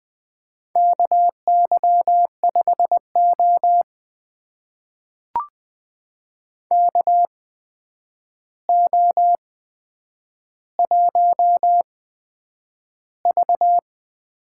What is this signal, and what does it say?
Morse code, heard as a keyed tone, about 20 wpm, spaced as for 9 wpm, 700 Hz. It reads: KY5O KO1V
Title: USA Call Signs - 4 Characters - 8x Character Spacing - 20wpm